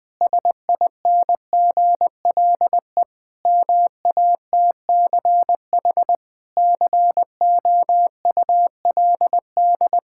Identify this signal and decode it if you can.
Morse code, heard as a keyed tone, about 20 wpm, 700 Hz. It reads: SINGLE MATCH COULD
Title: Sentences from Top 1600 Words 20wpm